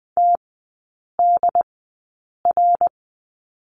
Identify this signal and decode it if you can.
Morse code, heard as a keyed tone, about 20 wpm, spaced as for 8 wpm, 700 Hz. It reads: TDR
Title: 159 Lesson 26e - Letter g - Three Characters 20wpm